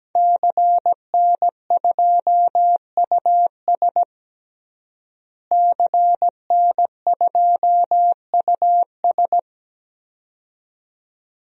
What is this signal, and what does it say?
Morse code, heard as a keyed tone, about 17 wpm, 700 Hz. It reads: CN2US CN2US
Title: DXCC-100 Call Signs - 5 Characters - Repeated 2x - 17wpm